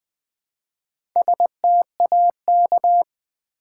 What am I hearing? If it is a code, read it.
Morse code, heard as a keyed tone, about 20 wpm, 700 Hz. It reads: STAK
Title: Build A Word - 1 Letter At A Time - 3 to 5-Letter Words - 20wpm